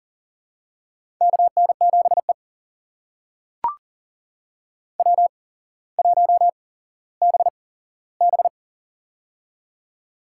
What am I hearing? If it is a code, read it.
Morse code, heard as a keyed tone, about 40 wpm, spaced as for 17 wpm, 700 Hz. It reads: KN7E W1BB
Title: USA Call Signs - 4 Characters - 8x Character Spacing - 40wpm